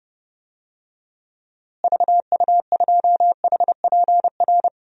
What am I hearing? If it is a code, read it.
Morse code, heard as a keyed tone, about 30 wpm, 700 Hz. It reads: VU2HPR
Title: DXCC-100 Call Signs - 6 Characters - Repeated 2x - 30wpm